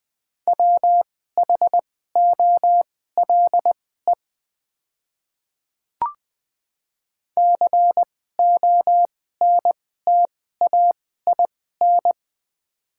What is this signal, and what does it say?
Morse code, heard as a keyed tone, about 20 wpm, spaced as for 15 wpm, 700 Hz. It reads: WHOLE CONTAIN
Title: 2x Character Spacing - Top 500 Words - 20wpm